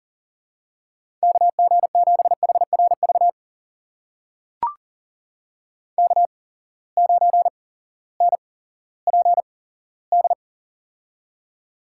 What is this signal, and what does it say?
Morse code, heard as a keyed tone, about 43 wpm, spaced as for 19 wpm, 700 Hz. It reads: KG7HRV K9NPD
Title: USA Call Signs - 8x Character Spacing - 40wpm